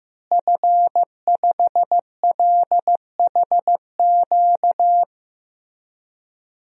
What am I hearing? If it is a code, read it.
Morse code, heard as a keyed tone, about 15 wpm, 700 Hz. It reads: F5LHQ
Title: DXCC-100 Call Signs - 5 Characters - Repeated 3x - 15wpm